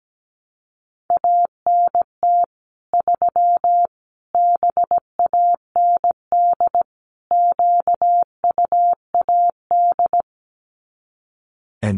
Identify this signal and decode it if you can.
Morse code, heard as a keyed tone, about 17 wpm, 700 Hz. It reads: ANT 3 BAND QUAD
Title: Common QSO Phrases 17wpm